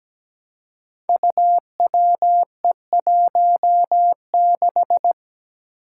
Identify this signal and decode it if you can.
Morse code, heard as a keyed tone, about 17 wpm, 700 Hz. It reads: UWE16
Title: Five Challenging Letter-Number Combinations 17wpm